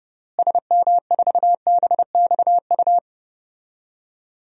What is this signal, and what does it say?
Morse code, heard as a keyed tone, about 30 wpm, 700 Hz. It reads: SM4BXU
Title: DXCC-100 Call Signs - 6 Characters - Repeated 3x - 30wpm